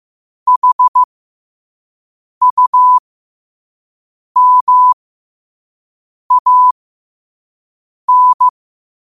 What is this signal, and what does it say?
Morse code, heard as a keyed tone, about 15 wpm, spaced as for 5 wpm, 1000 Hz. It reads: HUMAN
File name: qotdfiles / 5wpm